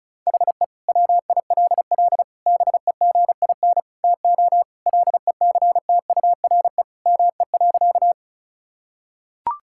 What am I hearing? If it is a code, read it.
Morse code, heard as a keyed tone, about 34 wpm, 700 Hz. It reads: HE WILL BEGIN TO LECTURE ME.
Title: Sentences from Top 1600 Words 35wpm